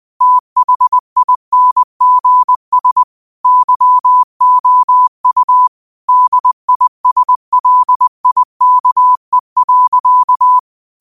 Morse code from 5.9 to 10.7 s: DISLIKE.

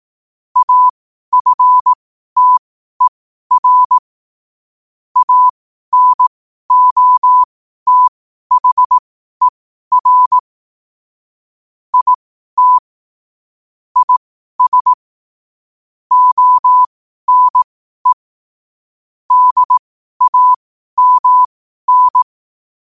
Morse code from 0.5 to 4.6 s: AFTER

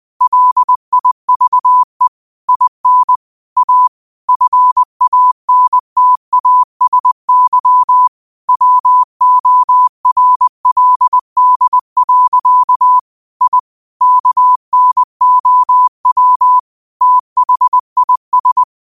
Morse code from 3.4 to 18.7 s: A FANTASY WORLD. I KNOW THIS

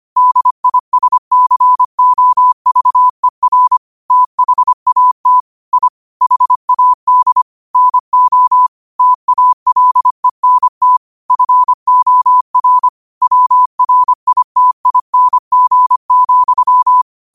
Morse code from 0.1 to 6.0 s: DISCOVER THAT I

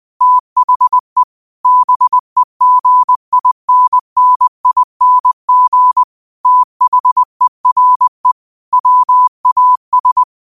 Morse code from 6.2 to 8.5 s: THERE

